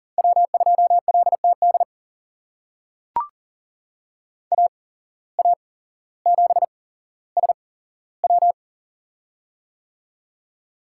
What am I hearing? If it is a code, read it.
Morse code, heard as a keyed tone, about 40 wpm, spaced as for 17 wpm, 700 Hz. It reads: W2LTD AA7SW
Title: USA Call Signs - 5 Characters - 8x Character Spacing - 40wpm